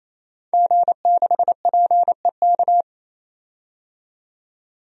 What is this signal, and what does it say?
Morse code, heard as a keyed tone, about 28 wpm, 700 Hz. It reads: G6PEK
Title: DXCC-100 Call Signs - 5 Characters - 28wpm